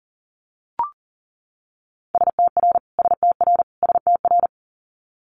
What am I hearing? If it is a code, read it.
Morse code, heard as a keyed tone, about 40 wpm, 700 Hz. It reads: STR STR STR
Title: Multi-letter Phonemes Repeated 3x 40wpm